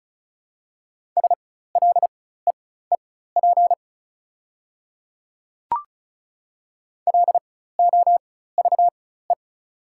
Morse code, heard as a keyed tone, about 34 wpm, spaced as for 16 wpm, 690 Hz. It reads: SLEEP LOVE
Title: 4x Character Spacing - Top 500 Words - 35wpm